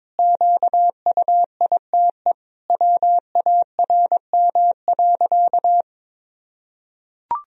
Morse code, heard as a keyed tone, about 22 wpm, 700 Hz. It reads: QUITE WARM.
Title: Sentences from Top 1000 Words - Review Part 4 - 22wpm